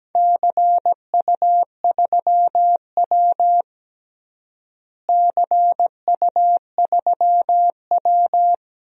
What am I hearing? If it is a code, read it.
Morse code, heard as a keyed tone, about 17 wpm, 700 Hz. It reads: CU3W CU3W
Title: DXCC-100 Call Signs - 4 Characters - Repeated 3x - 17wpm